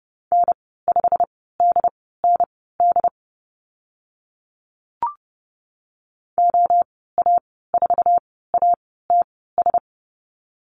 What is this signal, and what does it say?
Morse code, heard as a keyed tone, about 30 wpm, spaced as for 17 wpm, 700 Hz. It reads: N5DND OA4ATS
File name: Lesson 15g - Number 5 - Call Signs 30wpm